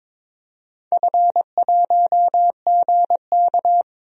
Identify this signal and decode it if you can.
Morse code, heard as a keyed tone, about 23 wpm, 700 Hz. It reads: F1GK
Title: DXCC-100 Call Signs - 4 Characters - 22wpm